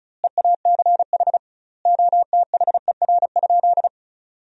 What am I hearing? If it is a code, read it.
Morse code, heard as a keyed tone, about 36 wpm, 700 Hz. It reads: EACH OTHER?
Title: 2x Word-Spacing - Sentences from Top 200 Words - 35wpm